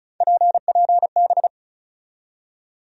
Morse code, heard as a keyed tone, about 34 wpm, 700 Hz. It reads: PPB
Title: USA Call Sign Postfixes - 3 Letters 35wpm